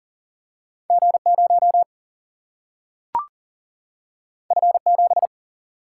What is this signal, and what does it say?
Morse code, heard as a keyed tone, about 40 wpm, 700 Hz. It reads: G0 F7
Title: Single-Letter Single-Number 40wpm